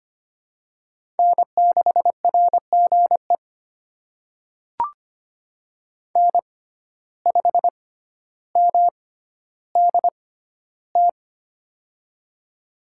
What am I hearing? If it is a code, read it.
Morse code, heard as a keyed tone, about 25 wpm, spaced as for 13 wpm, 700 Hz. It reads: N6RGE N5MDT
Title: USA Call Signs - 6x Character Spacing - 25wpm